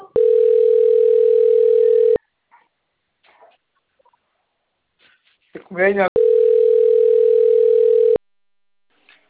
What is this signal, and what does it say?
Ringback tone (North America), 440 + 480 Hz, with speech.
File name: rg-1600-unknown- - - .50754